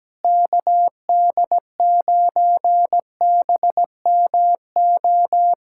Morse code, heard as a keyed tone, about 17 wpm, 700 Hz. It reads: KD9BMO
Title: USA Callsign DE Callsign - 6 Characters - 17wpm